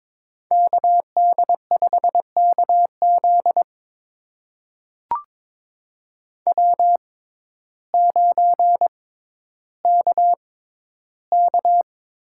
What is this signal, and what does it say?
Morse code, heard as a keyed tone, about 22 wpm, spaced as for 12 wpm, 700 Hz. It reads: KD5KZ W9KK
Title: USA Call Signs - 5 Characters - 6x Character Spacing - 22wpm